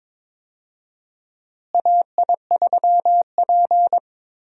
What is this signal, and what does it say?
Morse code, heard as a keyed tone, about 22 wpm, 700 Hz. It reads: AI3P
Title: USA Call Signs - 4 Characters - Repeated 3x - 22wpm